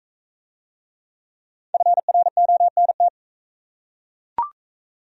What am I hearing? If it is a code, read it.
Morse code, heard as a keyed tone, about 39 wpm, 700 Hz. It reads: FRONT